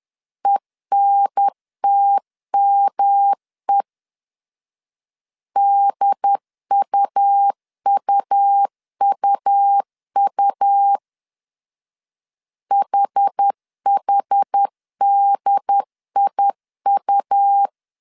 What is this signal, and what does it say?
Morse code, heard as a keyed tone, about 10 wpm, 800 Hz. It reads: ENTME DUUUU HHDIU
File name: CW / DET / 1 3 1500C S C 10wpm 3335